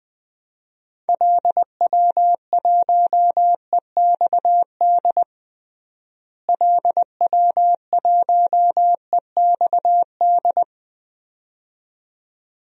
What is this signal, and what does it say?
Morse code, heard as a keyed tone, about 20 wpm, 700 Hz. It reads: LW1EXD LW1EXD
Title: DXCC-100 Call Signs - 6 Characters - Repeated 2x - 20wpm